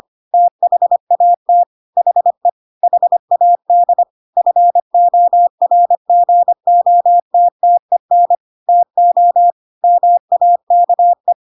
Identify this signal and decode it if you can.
Morse code, heard as a keyed tone, about 25 wpm, 700 Hz. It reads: THAT HE HAD FORGOTTEN TO MAKE
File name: Pinocchio / Pinocchio ch3 25 25 ik1hge